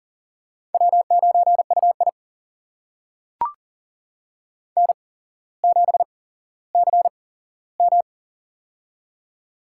Morse code, heard as a keyed tone, about 40 wpm, spaced as for 17 wpm, 700 Hz. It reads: W9UI N7CM
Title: USA Call Signs - 4 Characters - 8x Character Spacing - 40wpm